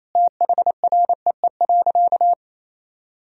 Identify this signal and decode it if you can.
Morse code, heard as a keyed tone, about 28 wpm, 700 Hz. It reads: THREE.
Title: Sentences from Top 1000 Words - Review Part 4 - 28wpm